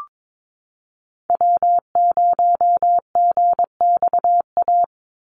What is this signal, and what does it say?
Morse code, heard as a keyed tone, about 22 wpm, 700 Hz. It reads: W0GXA
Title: Call Signs Repeated 2x - 22wpm